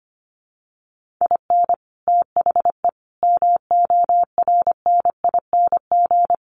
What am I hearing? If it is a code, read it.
Morse code, heard as a keyed tone, about 25 wpm, 700 Hz. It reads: IN THE MORNING
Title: Sets of 3 Words 25wpm